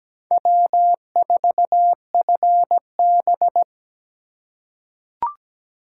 Morse code, heard as a keyed tone, about 17 wpm, 700 Hz. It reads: W4FB
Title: USA Call Signs - 4 Characters - 3x Character Spacing - 17wpm